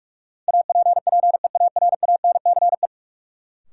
Morse code, heard as a keyed tone, about 46 wpm, 700 Hz. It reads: APPEARANCE
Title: Sentences from Top 1100 Words - New Word Drill Alt - 45wpm